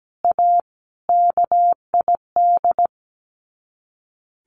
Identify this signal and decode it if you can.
Morse code, heard as a keyed tone, about 17 wpm, 700 Hz. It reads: A KID
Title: Sets of 5 Words 17wpm